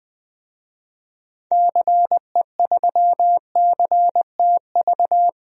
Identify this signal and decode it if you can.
Morse code, heard as a keyed tone, about 20 wpm, 700 Hz. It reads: CE3CTV
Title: DXCC-100 Call Signs - 6 Characters - 20wpm